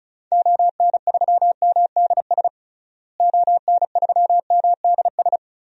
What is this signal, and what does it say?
Morse code, heard as a keyed tone, about 34 wpm, 700 Hz. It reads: ON3MDS ON3MDS
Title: DXCC-100 Call Signs - 6 Characters - Repeated 2x - 35wpm